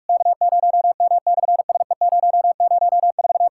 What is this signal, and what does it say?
Morse code, heard as a keyed tone, about 43 wpm, 690 Hz. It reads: K0M/SE004
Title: SOTA QSO Practice 45wpm